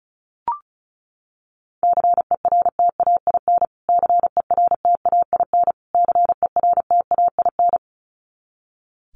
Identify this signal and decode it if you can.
Morse code, heard as a keyed tone, about 34 wpm, 700 Hz. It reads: CERTAIN CERTAIN CERTAIN
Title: Top 500 Words Repeated 35wpm